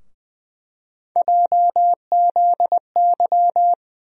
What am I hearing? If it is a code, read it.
Morse code, heard as a keyed tone, about 20 wpm, 700 Hz. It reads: JZY
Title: USA Call Sign Postfixes - 3 Letters 20wpm